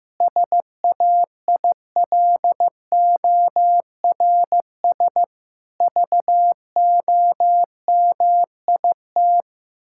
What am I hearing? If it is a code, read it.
Morse code, heard as a keyed tone, about 15 wpm, 700 Hz. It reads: SAILORS VOMIT
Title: Pangram Challenge v2 - 15wpm